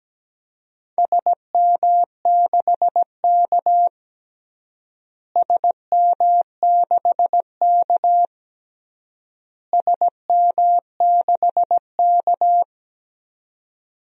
Morse code, heard as a keyed tone, about 17 wpm, 700 Hz. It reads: SM6K SM6K SM6K
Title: DXCC-100 Call Signs - 4 Characters - Repeated 3x - 17wpm